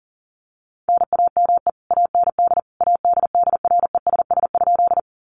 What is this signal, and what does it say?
Morse code, heard as a keyed tone, about 40 wpm, 700 Hz. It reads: NAME AND ADDRESS?
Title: Sentences from Top 1000 Words 40wpm